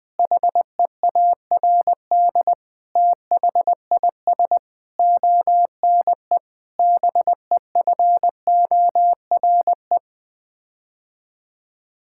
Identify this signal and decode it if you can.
Morse code, heard as a keyed tone, about 19 wpm, 700 Hz. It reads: HEARD THIS ONE BEFORE
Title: Sets of 8 Words - 20wpm